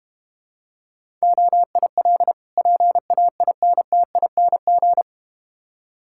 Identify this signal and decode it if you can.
Morse code, heard as a keyed tone, about 32 wpm, 700 Hz. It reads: OIL PAINTING